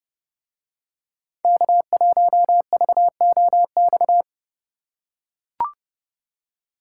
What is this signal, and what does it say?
Morse code, heard as a keyed tone, about 29 wpm, 700 Hz. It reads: K1VOX